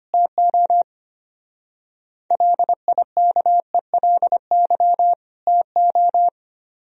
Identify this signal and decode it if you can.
Morse code, heard as a keyed tone, about 31 wpm, 700 Hz. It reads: TO LIKELY TO
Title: Speed-Racing - Sets of 2 Words 25wpm